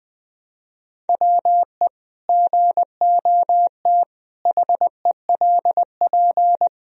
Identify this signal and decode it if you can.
Morse code, heard as a keyed tone, about 20 wpm, 700 Hz. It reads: WE GOT HELP
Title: Sentences from Top 1000 Words - Review Part 2 - 20wpm